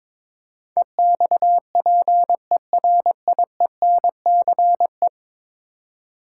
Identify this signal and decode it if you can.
Morse code, heard as a keyed tone, about 22 wpm, 700 Hz. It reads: EXPERIENCE
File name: practice / 22wpm / 10 Letter Words - First 3 Letters Given - 22wpm